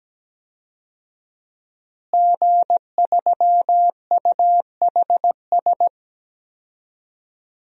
Morse code, heard as a keyed tone, about 17 wpm, 700 Hz. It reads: G3UHS
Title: DXCC-100 Call Signs - 5 Characters - Repeated 2x - 17wpm